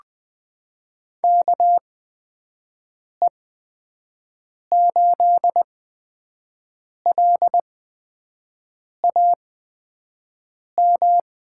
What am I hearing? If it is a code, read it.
Morse code, heard as a keyed tone, about 20 wpm, spaced as for 5 wpm, 700 Hz. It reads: KE8LAM
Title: USA Call Signs - 6 Characters - 8x Character Spacing - 20wpm